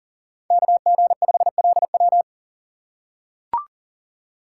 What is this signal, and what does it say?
Morse code, heard as a keyed tone, about 40 wpm, 700 Hz. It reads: KG5LW
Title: USA Call Signs - 5 Characters - 3x Character Spacing - 40wpm